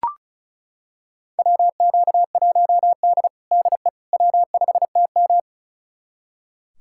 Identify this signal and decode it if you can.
Morse code, heard as a keyed tone, about 34 wpm, 700 Hz. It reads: WQ1D DE W5TM
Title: USA Callsign DE Callsign - 4 Characters - 35wpm